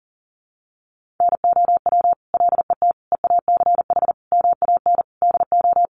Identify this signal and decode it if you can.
Morse code, heard as a keyed tone, about 40 wpm, 700 Hz. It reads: NOW LET EACH MAN DO
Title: Sentences from Top 1500 Words 40wpm